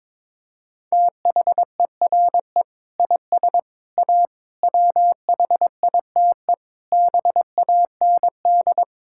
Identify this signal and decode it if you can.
Morse code, heard as a keyed tone, about 22 wpm, 700 Hz. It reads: THERE IS A WHITE BAND
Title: Sentences from Top 1600 Words 22wpm